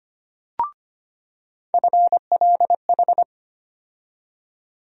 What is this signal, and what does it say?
Morse code, heard as a keyed tone, about 25 wpm, 700 Hz. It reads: FLH
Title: USA Call Sign Postfixes - 3 Letters 25wpm